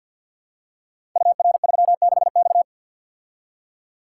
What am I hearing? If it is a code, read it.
Morse code, heard as a keyed tone, about 50 wpm, 700 Hz. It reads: UR36X
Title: Code Groups - 5 Letter:Numbers - 50wpm